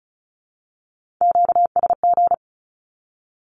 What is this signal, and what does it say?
Morse code, heard as a keyed tone, about 36 wpm, 700 Hz. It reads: QSG
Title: Q-Codes with Spoken Meaning - 35wpm